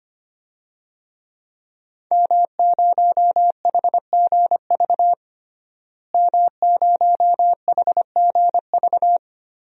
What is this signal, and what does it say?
Morse code, heard as a keyed tone, about 25 wpm, 700 Hz. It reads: M0HGV M0HGV
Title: DXCC-100 Call Signs - 5 Characters - Repeated 3x - 25wpm